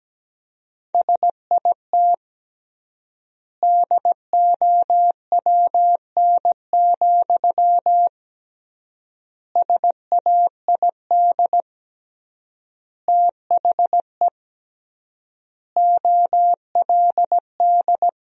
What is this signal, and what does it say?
Morse code, heard as a keyed tone, about 17 wpm, 700 Hz. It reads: SIT DOWN, SAID THE OLD
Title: 3x Word-Spacing - Sentences from Top 500 Words - 17wpm